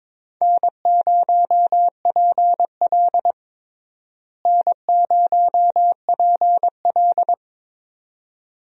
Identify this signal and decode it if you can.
Morse code, heard as a keyed tone, about 22 wpm, 700 Hz. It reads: N0PL N0PL
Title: USA Call Signs - 4 Characters - Repeated 2x - 22wpm